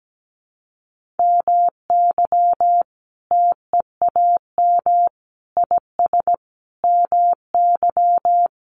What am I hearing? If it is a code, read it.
Morse code, heard as a keyed tone, about 17 wpm, 700 Hz. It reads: MY TEAM IS MY
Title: Sentences from Top 1500 Words 17wpm